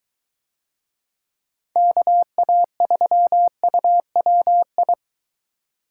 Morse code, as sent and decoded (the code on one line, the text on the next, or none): -.- .- ...-- ..- .-- ..
KA3UWI